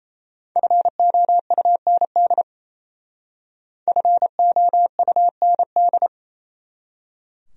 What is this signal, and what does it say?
Morse code, heard as a keyed tone, about 34 wpm, 700 Hz. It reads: FOUND FOUND
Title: Speed-Racing Top 500 Words 28wpm